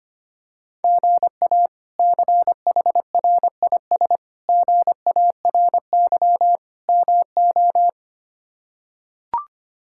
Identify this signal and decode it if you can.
Morse code, heard as a keyed tone, about 25 wpm, 700 Hz. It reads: GA CHRIS GARY MO
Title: K1USN SST QSO Practice - 25wpm